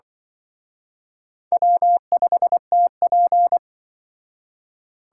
Morse code, heard as a keyed tone, about 24 wpm, 700 Hz. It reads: W5TP
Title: Speed Racing - USA Call Signs 20wpm